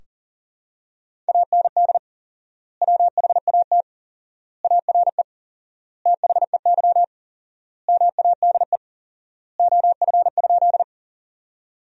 Morse code, heard as a keyed tone, about 40 wpm, 700 Hz. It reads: AND WHAT ARE THEY MADE OF?
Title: 4x Word-Spacing - Sentences from Top 200 Words - 40wpm